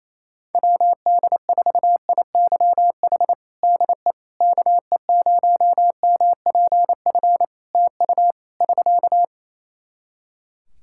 Morse code, heard as a keyed tone, about 28 wpm, 700 Hz. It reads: WD4IYH DE KE0MPF TU <SK>
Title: Practice QSO - Exchange 4 - Ending - 28wpm